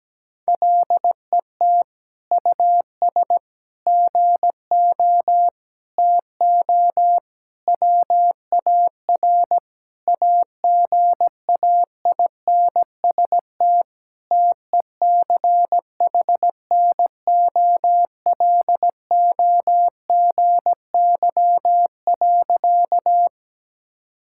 Morse code, heard as a keyed tone, about 17 wpm, 700 Hz. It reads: LET US GO TO WAR AGAINST TECHNOLOGY.
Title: Sentences from Top 1000 Words - Review Part 5 - 17wpm